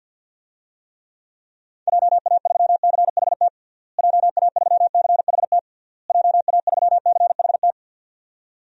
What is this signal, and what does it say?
Morse code, heard as a keyed tone, about 49 wpm, 700 Hz. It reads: JA3CHT JA3CHT JA3CHT
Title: DXCC-100 Call Signs - 6 Characters - Repeated 3x - 50wpm